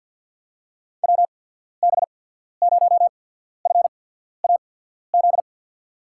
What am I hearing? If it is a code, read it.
Morse code, heard as a keyed tone, about 47 wpm, spaced as for 13 wpm, 700 Hz. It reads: WB0FAZ
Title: USA Call Signs - 6 Characters - 8x Character Spacing - 50wpm